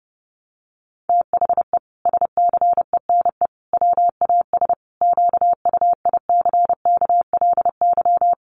Morse code, heard as a keyed tone, about 30 wpm, 700 Hz. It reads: THE SCENE WAS QUICKLY
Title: Sentences from Top 1100 Words 30wpm